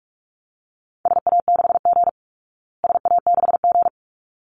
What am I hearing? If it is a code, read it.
Morse code, heard as a keyed tone, about 44 wpm, 700 Hz. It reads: SA6G SA6G
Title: Call Signs Repeated 3x - 45wpm